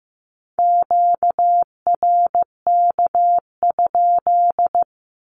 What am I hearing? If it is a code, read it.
Morse code, heard as a keyed tone, about 15 wpm, 700 Hz. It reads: QRK?
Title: Q-Codes with Spoken Meaning - 15wpm